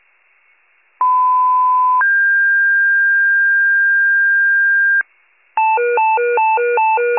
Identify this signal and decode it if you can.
Two-tone page, heard as a one-second tone, then a three-second tone, about 1010 then 1600 Hz. It is heard from 1.0 to 5.0 s.